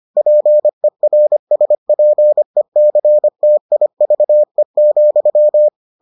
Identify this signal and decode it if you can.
Morse code, heard as a keyed tone, about 25 wpm, 600 Hz. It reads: PERSPECTIVE,